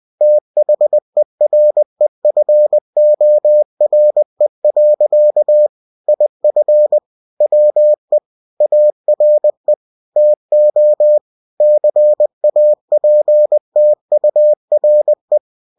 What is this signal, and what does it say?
Morse code, heard as a keyed tone, about 20 wpm, 600 Hz. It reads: THEREFORE. IF WE ARE TO CAPTURE
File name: cw-books / winnie-the-pooh / winnie-20wpm-page-0063